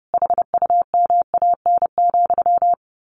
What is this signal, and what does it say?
Morse code, heard as a keyed tone, about 30 wpm, 700 Hz. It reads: HUMAN,
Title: A Sound Of Thunder 30wpm